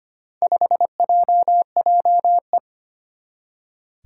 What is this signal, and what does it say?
Morse code, heard as a keyed tone, about 25 wpm, 700 Hz. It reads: 5JJE
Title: Four Challenging Letter-Number Combinations 25wpm